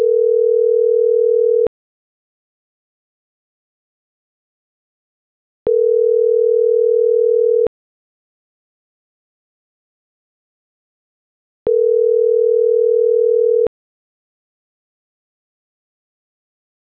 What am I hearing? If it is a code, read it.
Ringback tone (North America), 440 + 480 Hz.